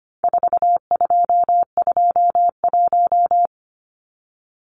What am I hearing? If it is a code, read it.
Morse code, heard as a keyed tone, about 25 wpm, 700 Hz. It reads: 4221
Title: Numbers - 4-Digit 25wpm